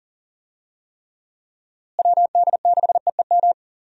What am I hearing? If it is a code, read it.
Morse code, heard as a keyed tone, about 40 wpm, 700 Hz. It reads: WD6EEM
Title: USA Call Signs - 6 Characters - Repeated 2x - 40wpm